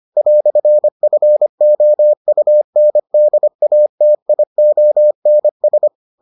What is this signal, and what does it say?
Morse code, heard as a keyed tone, about 25 wpm, 600 Hz. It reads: "FOUNDATIONS